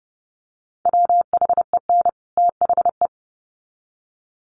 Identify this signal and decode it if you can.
Morse code, heard as a keyed tone, about 30 wpm, 700 Hz. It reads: WHEN THE
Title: Sets of 2 Words 30wpm